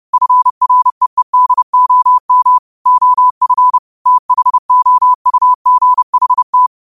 Morse code, heard as a keyed tone, about 30 wpm, 1000 Hz. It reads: FREEDOM OF THOUGHT